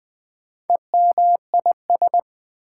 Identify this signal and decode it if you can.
Morse code, heard as a keyed tone, about 20 wpm, 700 Hz. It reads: EMIS
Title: Build A Word - 2 Letters At A Time - 8-Letter Words - 20wpm